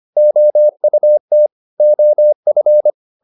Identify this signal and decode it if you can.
Morse code, heard as a keyed tone, about 25 wpm, 600 Hz. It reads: OUT OF
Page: advanced-practice-files